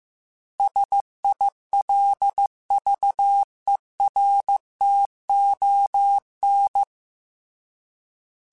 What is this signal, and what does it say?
Morse code, heard as a keyed tone, about 15 wpm, 800 Hz. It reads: SILVERTON